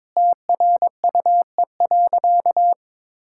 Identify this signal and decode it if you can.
Morse code, heard as a keyed tone, about 22 wpm, 700 Hz. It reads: TRUE.
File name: practice / 22wpm / Sentences with 3-Word Phrases - 22wpm